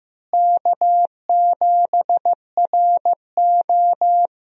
Morse code, heard as a keyed tone, about 15 wpm, 700 Hz. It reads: K7RO
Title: USA Call Signs - 4 Characters - Repeated 2x - 15wpm